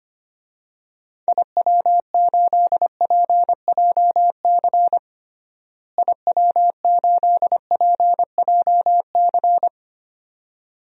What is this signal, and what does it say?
Morse code, heard as a keyed tone, about 25 wpm, 700 Hz. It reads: IW8PJC IW8PJC
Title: DXCC-100 Call Signs - 6 Characters - Repeated 2x - 25wpm